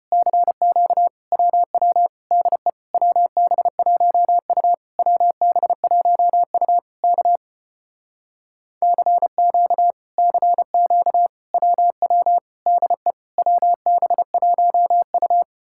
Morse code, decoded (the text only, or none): CQ WW DE WB1U WB1U K CQ CQ WW DE WB1U